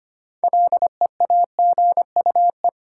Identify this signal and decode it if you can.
Morse code, heard as a keyed tone, about 25 wpm, 700 Hz. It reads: LEAGUE
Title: Six Letter Words - 25wpm